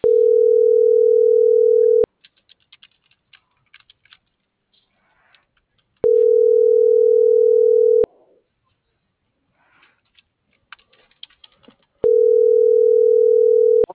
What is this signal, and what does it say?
Ringback tone (North America), 440 + 480 Hz.